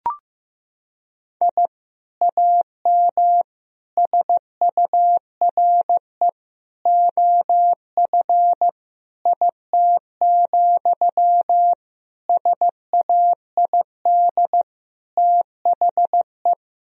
Morse code, heard as a keyed tone, about 15 wpm, 700 Hz. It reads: I AM SURE OF IT, SAID THE
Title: Sentences from Top 1000 Words - Review Part 4 - 15wpm